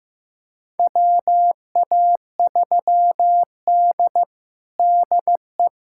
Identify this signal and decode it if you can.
Morse code, heard as a keyed tone, about 15 wpm, 700 Hz. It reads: WA3D DE
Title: USA Callsign DE Callsign - 4 Characters - 15wpm